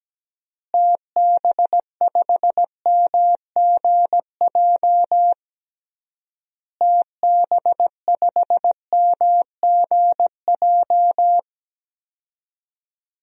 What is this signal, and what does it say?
Morse code, heard as a keyed tone, about 17 wpm, 700 Hz. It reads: TB5MGJ TB5MGJ
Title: DXCC-100 Call Signs - 6 Characters - Repeated 3x - 17wpm